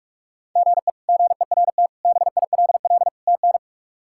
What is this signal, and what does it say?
Morse code, heard as a keyed tone, about 42 wpm, 700 Hz. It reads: GE GERT BILL TN
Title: K1USN SST QSO Practice - 45wpm